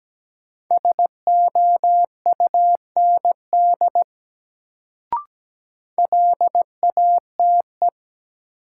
Morse code, heard as a keyed tone, about 17 wpm, 700 Hz. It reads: SOUND LATE
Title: Sets of 3 Words - New Word Drill - 17wpm